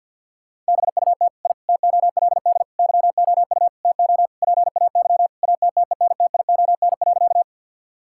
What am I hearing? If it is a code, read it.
Morse code, heard as a keyed tone, about 50 wpm, 700 Hz. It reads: BUT I TOLD YOU TO PAY ATTENTION.
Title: Sentences from Top 1000 Words - Review Part 6 - 50wpm